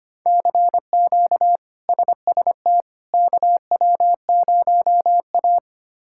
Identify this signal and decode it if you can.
Morse code, heard as a keyed tone, about 25 wpm, 700 Hz. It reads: CQ SST KW0A
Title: K1USN SST QSO Practice - 25wpm